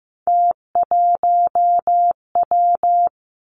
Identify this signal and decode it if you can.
Morse code, heard as a keyed tone, about 15 wpm, 700 Hz. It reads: T1W
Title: Three Challenging Letter-Number Combinations 15wpm